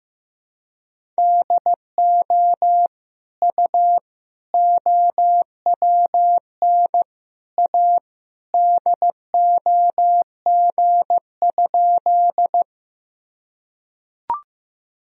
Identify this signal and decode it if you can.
Morse code, heard as a keyed tone, about 15 wpm, 700 Hz. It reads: DO U OWN A DOG?
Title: Sentences from Top 400 Words and Abbreviations 15wpm